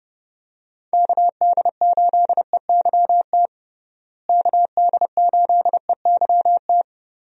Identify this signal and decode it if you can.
Morse code, heard as a keyed tone, about 32 wpm, 700 Hz. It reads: KD8EYT KD8EYT
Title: USA Call Signs - 6 Characters - Repeated 3x - 30wpm